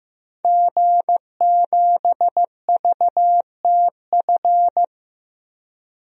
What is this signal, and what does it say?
Morse code, heard as a keyed tone, about 15 wpm, 700 Hz. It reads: G7VTF
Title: DXCC-100 Call Signs - 5 Characters - 15wpm